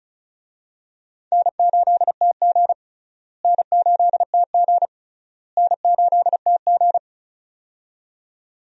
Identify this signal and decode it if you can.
Morse code, heard as a keyed tone, about 34 wpm, 700 Hz. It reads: N8TG N8TG N8TG
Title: USA Call Signs - 4 Characters - Repeated 3x - 35wpm